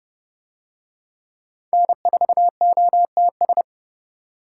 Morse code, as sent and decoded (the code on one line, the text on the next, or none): -. ....- --- - ...
N4OTS